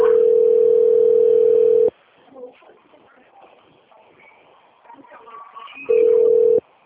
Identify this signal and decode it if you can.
Ringback tone (North America), 440 + 480 Hz, with speech.
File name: out- - - - - .58493